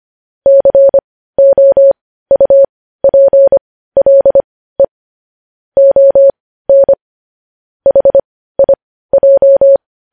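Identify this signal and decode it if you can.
Morse code, heard as a keyed tone, about 25 wpm, spaced as for 15 wpm, 550 Hz. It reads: COUPLE ON HIJ
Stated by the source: News headlines sent in Morse code at 15 wpm.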